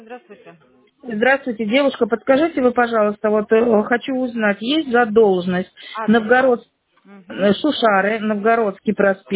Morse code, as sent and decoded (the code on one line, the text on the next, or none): none